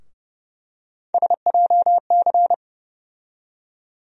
Morse code, heard as a keyed tone, about 30 wpm, 700 Hz. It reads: SJC
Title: USA Call Sign Postfixes - 3 Letters 30wpm